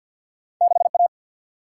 Morse code, heard as a keyed tone, about 53 wpm, 700 Hz. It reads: 6 A
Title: Single-Letter Single-Number 50wpm